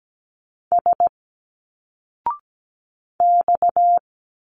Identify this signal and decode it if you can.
Morse code, heard as a keyed tone, about 17 wpm, 700 Hz. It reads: S X
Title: Single Letters 17wpm